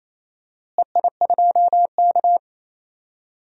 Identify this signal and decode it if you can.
Morse code, heard as a keyed tone, about 28 wpm, 700 Hz. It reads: EI2K
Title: DXCC-100 Call Signs - 4 Characters - Repeated 3x - 28wpm